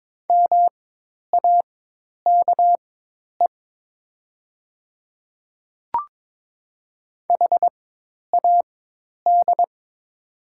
Morse code, heard as a keyed tone, about 21 wpm, spaced as for 10 wpm, 700 Hz. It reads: MAKE HAD
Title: 4x Character Spacing - Top 500 Words - 22wpm